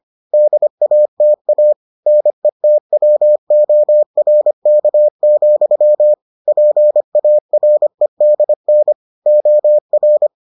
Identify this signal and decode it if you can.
Morse code, heard as a keyed tone, about 25 wpm, 600 Hz. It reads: DATA NETWORK, PAREDN OR